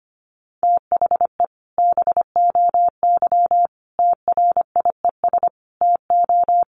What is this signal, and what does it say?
Morse code, heard as a keyed tone, about 25 wpm, 700 Hz. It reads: THE BOY TRIES TO
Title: Additional Easy Phrases 25wpm